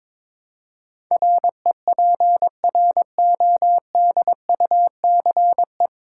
Morse code, heard as a keyed tone, about 21 wpm, 700 Hz. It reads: REPRODUCE
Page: Morse Code Ninja practice 22wpm